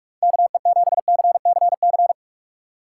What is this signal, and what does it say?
Morse code, heard as a keyed tone, about 44 wpm, 700 Hz. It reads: KE7CCC